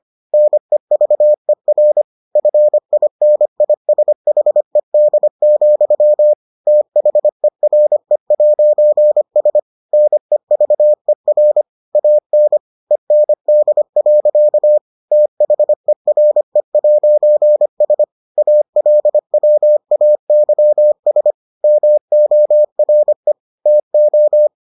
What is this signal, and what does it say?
Morse code, heard as a keyed tone, about 25 wpm, 600 Hz. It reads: NEVER FINISHED, THERE'S NEVER AN END. THERE'S ALWAYS MORE TO